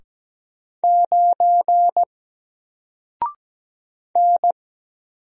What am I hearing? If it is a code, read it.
Morse code, heard as a keyed tone, about 17 wpm, 700 Hz. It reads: 9 N
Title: Single Letter-Number Alt 17wpm